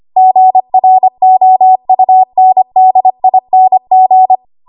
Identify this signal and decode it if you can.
Morse code, heard as a keyed tone, about 25 wpm, 750 Hz. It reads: GROUNDING
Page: 25 WPM Code Archive